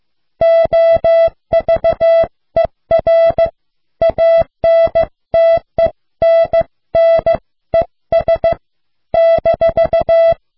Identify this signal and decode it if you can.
Morse code, heard as a keyed tone, about 16 wpm, 660 Hz. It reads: OVER ANTENNES -